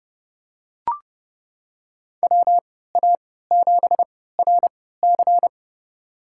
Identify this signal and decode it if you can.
Morse code, heard as a keyed tone, about 30 wpm, spaced as for 17 wpm, 700 Hz. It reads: WA7RC
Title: USA Call Signs - 5 Characters - 3x Character Spacing - 30wpm